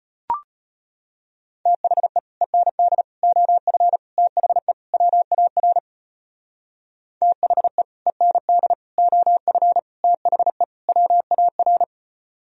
Morse code, heard as a keyed tone, about 30 wpm, 700 Hz. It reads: THE END OF THE WAR THE END OF THE WAR
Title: Speed-Racing - Sets of 5 Words 25wpm